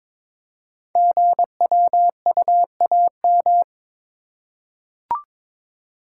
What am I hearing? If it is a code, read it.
Morse code, heard as a keyed tone, about 22 wpm, 700 Hz. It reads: GWUAM
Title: Code Groups - 5 Letters - 22wpm